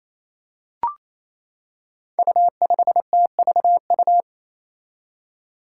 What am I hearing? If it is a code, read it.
Morse code, heard as a keyed tone, about 28 wpm, 700 Hz. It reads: U5TVU
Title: Five Challenging Letter-Number Combinations 28wpm